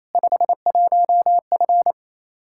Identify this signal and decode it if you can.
Morse code, heard as a keyed tone, about 28 wpm, 700 Hz. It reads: 51F
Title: Practice QSO - Exchange 2 - Info - 28wpm